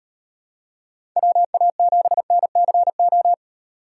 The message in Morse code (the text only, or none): WA7NCO